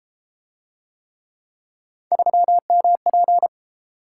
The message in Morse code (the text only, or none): 3MP